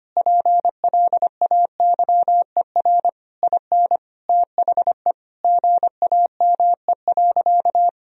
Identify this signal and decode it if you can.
Morse code, heard as a keyed tone, about 25 wpm, 700 Hz. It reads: PLAYER IN THE GAME.
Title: Sentences from Top 1000 Words - Review Part 4 - 25wpm